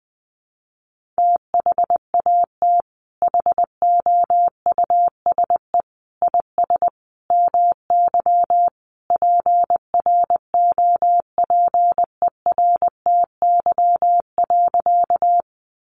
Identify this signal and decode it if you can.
Morse code, heard as a keyed tone, about 20 wpm, 700 Hz. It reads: THAT HOUSE IS MY PROPERTY.
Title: Sentences from Top 1000 Words 20wpm